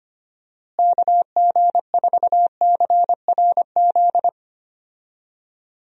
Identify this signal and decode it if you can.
Morse code, heard as a keyed tone, about 25 wpm, 700 Hz. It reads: KG4CRZ
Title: USA Call Signs - 6 Characters - Repeated 2x - 25wpm